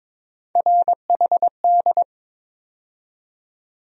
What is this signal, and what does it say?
Morse code, heard as a keyed tone, about 22 wpm, 700 Hz. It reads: RHD
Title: USA Call Sign Postfixes - 3 Letters 22wpm